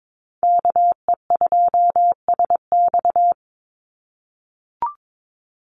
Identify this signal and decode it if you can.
Morse code, heard as a keyed tone, about 22 wpm, 700 Hz. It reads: KE2SX
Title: Call Signs - 22wpm